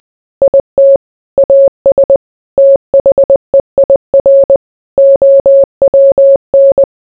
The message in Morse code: IT AS THEIR OWN

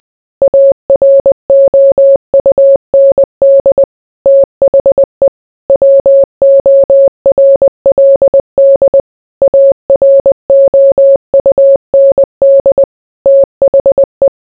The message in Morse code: AROUND THE WORLD AROUND THE

